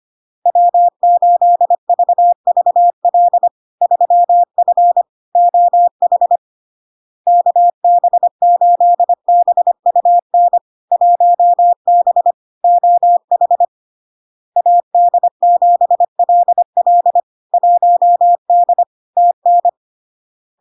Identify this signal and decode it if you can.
Morse code, heard as a keyed tone, about 25 wpm, 700 Hz. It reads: W8VVL 3F OH KB8BUN 1B OH AD7LL 1D TN
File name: LARK-FD-2022-Log-in-CW-25wpm-700-Hz